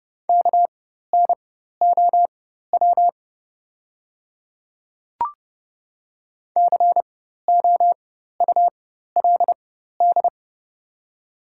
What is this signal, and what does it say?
Morse code, heard as a keyed tone, about 30 wpm, spaced as for 14 wpm, 700 Hz. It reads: KNOW COULD
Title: 4x Character Spacing - Top 100 Words - 30wpm